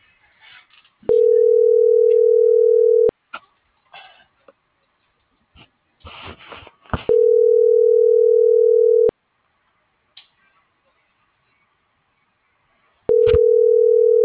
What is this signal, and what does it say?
Ringback tone (North America), 440 + 480 Hz.